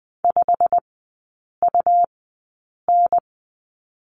Morse code, heard as a keyed tone, about 20 wpm, spaced as for 8 wpm, 700 Hz. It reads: 5UN
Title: 138 Lesson 23e - Letter f - Three Characters 20wpm